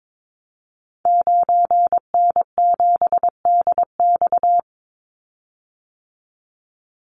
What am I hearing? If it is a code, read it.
Morse code, heard as a keyed tone, about 22 wpm, 700 Hz. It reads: 9N7DX
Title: Call Signs Repeated 3x - 22wpm